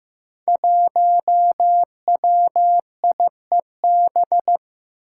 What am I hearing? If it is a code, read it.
Morse code, heard as a keyed tone, about 15 wpm, 700 Hz. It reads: 1WIEB
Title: Five Challenging Letter-Number Combinations 15wpm